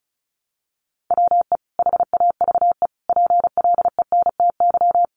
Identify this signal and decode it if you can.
Morse code, heard as a keyed tone, about 34 wpm, 700 Hz. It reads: WE HAVE PLENTY
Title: Sentences from Top 1300 Words 35wpm